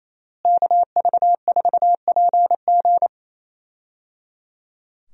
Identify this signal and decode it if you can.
Morse code, heard as a keyed tone, about 28 wpm, 700 Hz. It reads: KV4PG
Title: USA Call Signs - 5 Characters - Repeated 3x - 28wpm